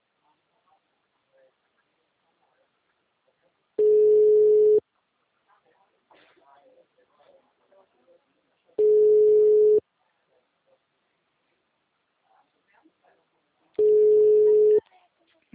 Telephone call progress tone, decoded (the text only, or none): ringback tone